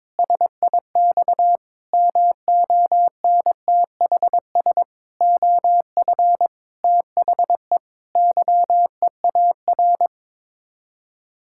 Morse code, heard as a keyed tone, about 21 wpm, 700 Hz. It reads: SIX MONTHS OF THE YEAR
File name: practice / 22wpm / Sets of 8 Words - 22wpm